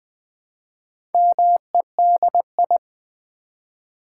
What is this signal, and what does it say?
Morse code, heard as a keyed tone, about 20 wpm, 700 Hz. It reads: MEDI
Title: Build A Word - 1 Letter At A Time - 5-Letter Words - 20wpm